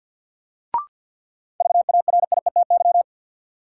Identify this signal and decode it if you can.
Morse code, heard as a keyed tone, about 51 wpm, 700 Hz. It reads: VARIETY
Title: Sentences from Top 1000 Words - Review Part 4 - New Word Drill - 50wpm